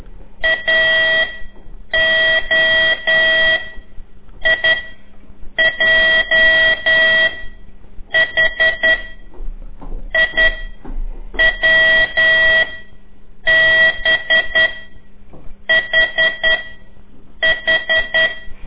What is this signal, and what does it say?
Morse code, heard as a keyed tone, about 10 wpm, spaced as for 7 wpm, 1950 Hz. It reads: AO I J H I W B H H
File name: CORSI CW